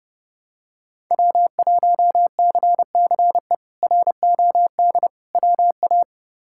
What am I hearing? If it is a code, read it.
Morse code, heard as a keyed tone, about 30 wpm, 700 Hz. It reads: W1CCE ROD WA
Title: K1USN SST QSO Practice - 30wpm